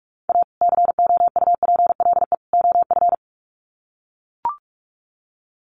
Morse code, heard as a keyed tone, about 44 wpm, 700 Hz. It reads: A COUPLE OF
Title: Sets of 3 Words 45wpm